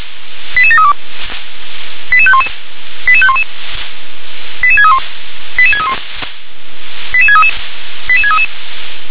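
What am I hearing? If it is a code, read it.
Five-tone selective call (ZVEI), heard as quick runs of five tones, tones of about 70 ms. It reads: 88732 88511 88511 88631 88631 88633 88633
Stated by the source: ZVEI five-tone call 88712 received with an SDR.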